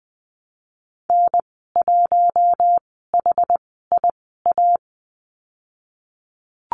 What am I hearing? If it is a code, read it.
Morse code, heard as a keyed tone, about 20 wpm, spaced as for 15 wpm, 700 Hz. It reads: N1HIA
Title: 069 Lesson 12g - Letter d - Call Signs 20wpm